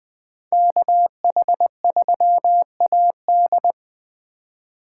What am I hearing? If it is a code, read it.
Morse code, heard as a keyed tone, about 20 wpm, 700 Hz. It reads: KH3AD
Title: Speed Racing - USA 2x2 Call Signs 17wpm